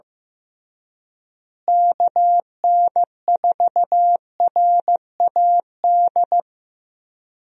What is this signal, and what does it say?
Morse code, heard as a keyed tone, about 15 wpm, 700 Hz. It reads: KN4RAD
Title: USA Call Signs - 6 Characters - Repeated 2x - 15wpm